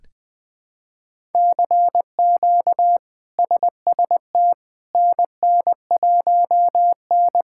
Morse code, heard as a keyed tone, about 21 wpm, 700 Hz. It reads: CQ SST NN1N
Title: K1USN SST QSO Practice - 20wpm